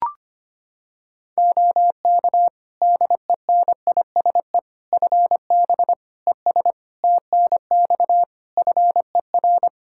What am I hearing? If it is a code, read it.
Morse code, heard as a keyed tone, about 25 wpm, 700 Hz. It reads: OK DENISE FB ES TNX FER
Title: Practice QSO - Exchange 2 - Info - 25wpm